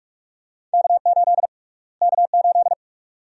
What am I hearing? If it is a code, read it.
Morse code, heard as a keyed tone, about 46 wpm, 700 Hz. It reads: K8 K8
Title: USA Call Sign Prefixes - Repeated 2x - 45wpm